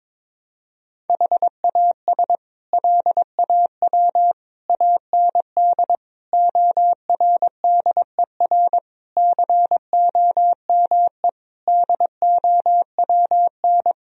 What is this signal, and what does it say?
Morse code, heard as a keyed tone, about 21 wpm, 700 Hz. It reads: HAS LAW AND ORDER COME DOWN
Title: Sentences from Top 1000 Words - Review Part 3 - 22wpm